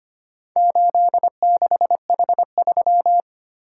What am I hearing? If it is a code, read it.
Morse code, heard as a keyed tone, about 25 wpm, 700 Hz. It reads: 86H3
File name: practice / 25wpm / Four Letter-Number Combinations - 25wpm